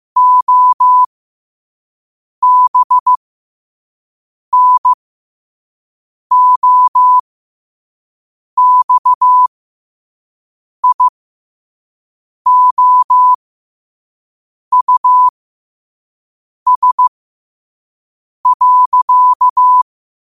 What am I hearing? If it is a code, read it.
Morse code, heard as a keyed tone, about 15 wpm, spaced as for 5 wpm, 1000 Hz. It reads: OBNOXIOUS.
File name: qotdfiles / 5wpm